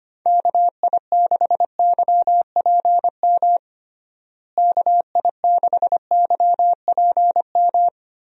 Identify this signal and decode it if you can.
Morse code, heard as a keyed tone, about 25 wpm, 700 Hz. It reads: KI6YPM KI6YPM
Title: USA Call Signs - 6 Characters - Repeated 3x - 25wpm